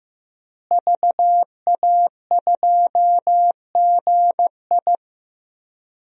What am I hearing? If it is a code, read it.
Morse code, heard as a keyed tone, about 15 wpm, 700 Hz. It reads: VA2GI
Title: DXCC-100 Call Signs - 5 Characters - Repeated 2x - 15wpm